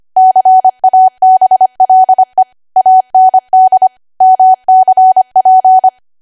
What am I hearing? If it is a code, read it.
Morse code, heard as a keyed tone, about 25 wpm, 750 Hz. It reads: CABLE AND MCP
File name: Archive / 25 WPM / 25WPM